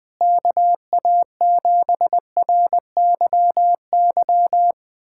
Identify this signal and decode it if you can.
Morse code, heard as a keyed tone, about 20 wpm, 700 Hz. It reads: KA7RYY
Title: USA Callsign DE Callsign - 6 Characters - 20wpm